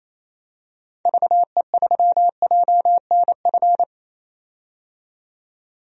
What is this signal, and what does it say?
Morse code, heard as a keyed tone, about 28 wpm, 700 Hz. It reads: VE3JNF
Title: DXCC-100 Call Signs - 6 Characters - 28wpm